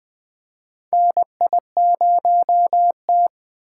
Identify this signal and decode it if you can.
Morse code, heard as a keyed tone, about 20 wpm, 700 Hz. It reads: NI0T
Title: USA Call Signs - 6x Character Spacing - 20wpm